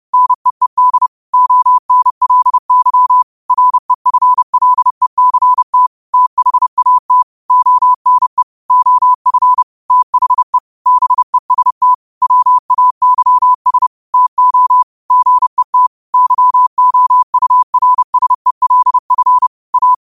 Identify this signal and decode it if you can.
Morse code, heard as a keyed tone, about 31 wpm, 1000 Hz. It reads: NEED ONLY REFLECT THAT ONE OF THE BEST WAYS TO GET YOURSELF A
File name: qotdfiles / 30wpm